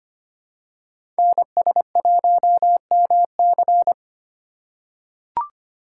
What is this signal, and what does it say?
Morse code, heard as a keyed tone, about 25 wpm, 700 Hz. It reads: NS1MC
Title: USA Call Signs - 5 Characters - 2x Character Spacing - 25wpm